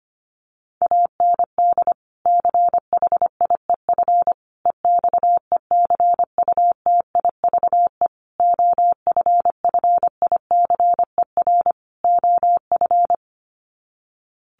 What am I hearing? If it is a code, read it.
Morse code, heard as a keyed tone, about 25 wpm, 700 Hz. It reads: AND CHIEF EXECUTIVE OFFICER OF
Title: Sets of 5 Words Encore2 25wpm